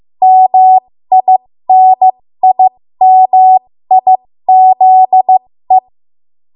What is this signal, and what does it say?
Morse code, heard as a keyed tone, about 15 wpm, spaced as for 13 wpm, 750 Hz. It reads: MINIMIZE
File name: Morse / 13WPM